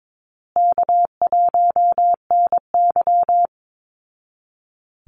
Morse code, heard as a keyed tone, about 22 wpm, 700 Hz. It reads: K1NY
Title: Easy Call Signs Repeated 2x - 22wpm